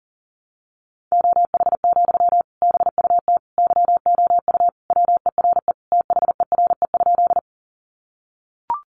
Morse code, heard as a keyed tone, about 40 wpm, 700 Hz. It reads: OH, BUT YOU WERE THERE?
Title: Sentences from Top 500 Words 40wpm